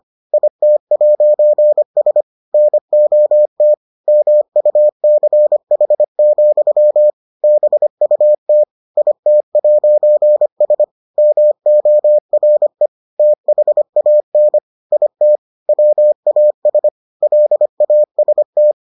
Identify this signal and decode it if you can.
Morse code, heard as a keyed tone, about 25 wpm, 600 Hz. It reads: IT'S NOT MUCH, BUT IT'S MORE THAN IT WAS LAST